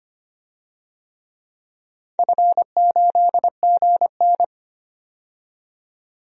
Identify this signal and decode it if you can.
Morse code, heard as a keyed tone, about 25 wpm, 700 Hz. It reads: F8GN
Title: DXCC-100 Call Signs - 4 Characters - 25wpm